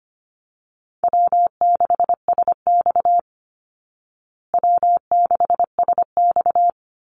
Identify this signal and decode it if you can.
Morse code, heard as a keyed tone, about 25 wpm, 700 Hz. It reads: W6SX W6SX
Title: Easy Call Signs Repeated 2x - 25wpm